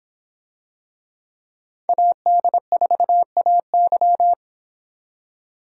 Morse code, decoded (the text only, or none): AD4AY